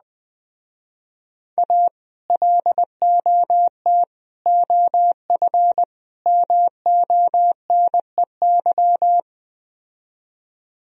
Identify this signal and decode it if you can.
Morse code, heard as a keyed tone, about 20 wpm, 700 Hz. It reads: A LOT OF MONEY
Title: Speed-Racing - Sets of 4 Words 17wpm